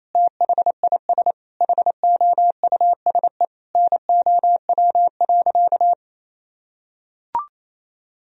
Morse code, heard as a keyed tone, about 27 wpm, 700 Hz. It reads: THIS HOUSE NOW.
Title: Sentences from Top 1000 Words - Review Part 2 - 28wpm